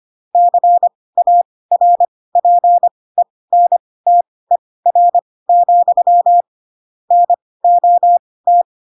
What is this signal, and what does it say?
Morse code, heard as a keyed tone, about 25 wpm, spaced as for 18 wpm, 700 Hz. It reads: CARPENTER, NOT
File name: Pinocchio / Pinocchio ch2 12 25 ik1hge